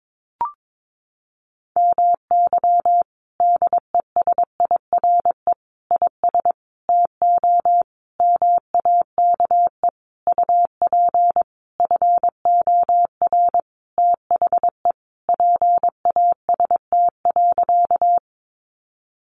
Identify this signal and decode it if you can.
Morse code, heard as a keyed tone, about 22 wpm, 700 Hz. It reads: MY DESIRE IS TO MAKE UP FOR THE PAST.
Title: Sentences from Top 1100 Words 22wpm